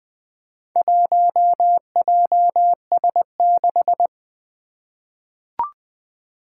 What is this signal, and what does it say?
Morse code, heard as a keyed tone, about 20 wpm, 700 Hz. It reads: 1JS6
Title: Four Challenging Letter-Number Combinations 20wpm